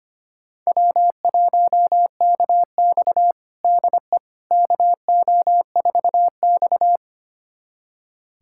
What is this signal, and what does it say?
Morse code, heard as a keyed tone, about 25 wpm, 700 Hz. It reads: W1KX DE KO4X
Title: USA Callsign DE Callsign - 4 Characters - 25wpm